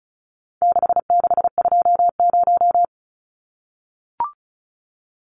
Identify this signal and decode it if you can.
Morse code, heard as a keyed tone, about 34 wpm, 700 Hz. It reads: 6620
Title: Numbers - 4-Digit 35wpm